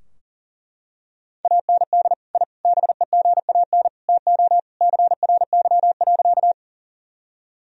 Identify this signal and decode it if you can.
Morse code, heard as a keyed tone, about 40 wpm, 700 Hz. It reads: AND I BEGAN TO CRY.
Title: Sentences from Top 1000 Words - Review Part 2 - 40wpm